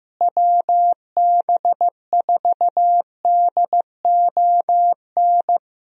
Morse code, heard as a keyed tone, about 15 wpm, 700 Hz. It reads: WB4DON
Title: USA Call Signs - 6 Characters - Repeated 2x - 15wpm